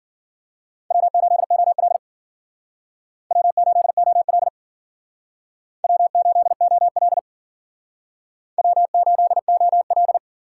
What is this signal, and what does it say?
Morse code, heard as a keyed tone, about 52 wpm, 690 Hz. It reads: W8OL W8OL W8OL W8OL